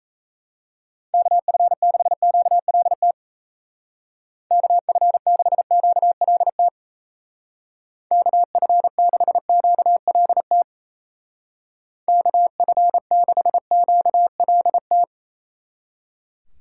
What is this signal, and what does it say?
Morse code, heard as a keyed tone, about 34 wpm, 700 Hz. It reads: KF6QLT KF6QLT KF6QLT KF6QLT
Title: Speed Racing - USA Call Signs 28wpm